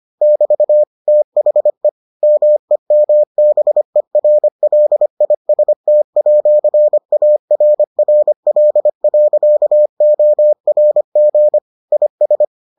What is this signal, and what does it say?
Morse code, heard as a keyed tone, about 25 wpm, 600 Hz. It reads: <BT> THE MEMBERLIST@ARRL.ORG IS